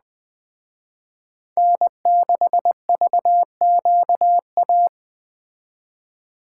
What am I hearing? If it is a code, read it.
Morse code, heard as a keyed tone, about 21 wpm, 700 Hz. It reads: N6VQA